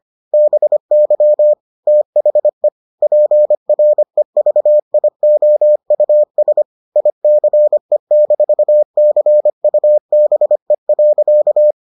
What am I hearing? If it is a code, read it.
Morse code, heard as a keyed tone, about 25 wpm, 600 Hz. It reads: BY THE PREVIOUS ICE-CUBE.